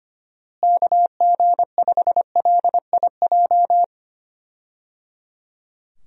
Morse code, heard as a keyed tone, about 25 wpm, 700 Hz. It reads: KG5LIJ
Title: USA Call Signs - 6 Characters - Repeated 3x - 25wpm